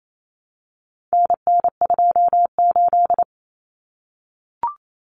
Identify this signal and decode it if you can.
Morse code, heard as a keyed tone, about 28 wpm, 700 Hz. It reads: NN28
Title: Common Cut Numbers - 4-Digit 28wpm